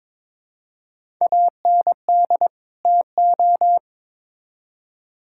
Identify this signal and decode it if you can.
Morse code, heard as a keyed tone, about 22 wpm, 700 Hz. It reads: AND TO
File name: Build A Phrase - 2-Word Phrases - 22wpm